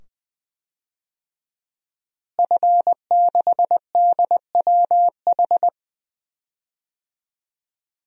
Morse code, heard as a keyed tone, about 20 wpm, 700 Hz. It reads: F6DWH
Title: DXCC-100 Call Signs - 5 Characters - Repeated 2x - 20wpm